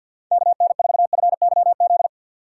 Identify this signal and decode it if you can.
Morse code, heard as a keyed tone, about 54 wpm, 700 Hz. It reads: KN4FYZ
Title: USA Call Signs - 6 Characters - 8x Character Spacing - 50wpm